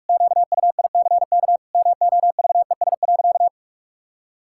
Morse code, heard as a keyed tone, about 44 wpm, 700 Hz. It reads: QUICK MOVES.
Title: Pangram Challenge v2 - 45wpm